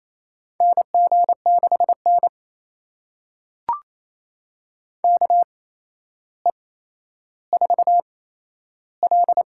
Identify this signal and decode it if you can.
Morse code, heard as a keyed tone, about 28 wpm, spaced as for 12 wpm, 700 Hz. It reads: NG6N KE4L
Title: USA Call Signs - 8x Character Spacing - 28wpm